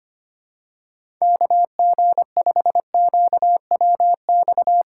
Morse code, heard as a keyed tone, about 25 wpm, 700 Hz. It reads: KG5QWX
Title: USA Callsign DE Callsign - 6 Characters - 25wpm